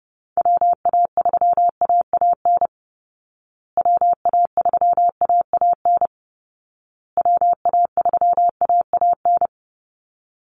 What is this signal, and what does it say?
Morse code, heard as a keyed tone, about 30 wpm, 700 Hz. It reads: WA3AAN WA3AAN WA3AAN
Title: Call Signs Repeated 2x - 30wpm